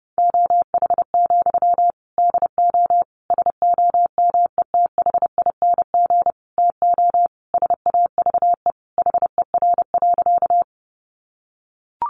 Morse code, heard as a keyed tone, about 30 wpm, 700 Hz. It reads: OH, DO SOMETHING TO SAVE HER.
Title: Sentences from Top 1100 Words 30wpm